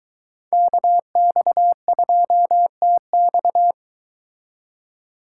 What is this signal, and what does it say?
Morse code, heard as a keyed tone, about 23 wpm, 700 Hz. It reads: KX2TX